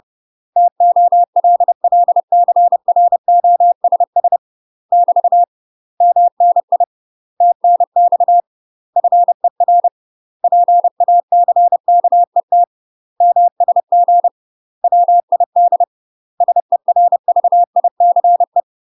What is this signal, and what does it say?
Morse code, heard as a keyed tone, about 30 wpm, 700 Hz. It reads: TOLLCROSS <BT> MNI TNX FER PACKET MSG WID SERVICE
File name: UK MASTER QSOs TRAINING / 30WPM / UK MASTER QSOs 0008